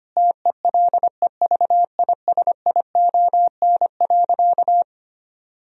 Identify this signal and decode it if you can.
Morse code, heard as a keyed tone, about 25 wpm, 700 Hz. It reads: TELEVISION.
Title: Sentences from Top 1000 Words - Review Part 4 - 25wpm